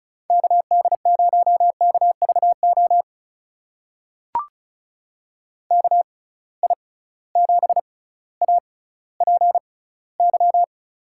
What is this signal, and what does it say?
Morse code, heard as a keyed tone, about 36 wpm, spaced as for 18 wpm, 700 Hz. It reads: KD0KVO KI7APY
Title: USA Call Signs - 6 Characters - 6x Character Spacing - 35wpm